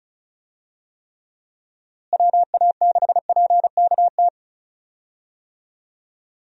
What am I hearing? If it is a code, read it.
Morse code, heard as a keyed tone, about 34 wpm, 700 Hz. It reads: WA6PKT
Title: USA Call Signs - 6 Characters - Repeated 2x - 35wpm